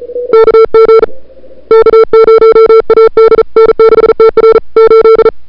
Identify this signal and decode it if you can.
Morse code, heard as a keyed tone, about 37 wpm, 440 Hz. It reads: KG K0AD N6TR 8
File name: n6tr / Fab2007CWSprintHour1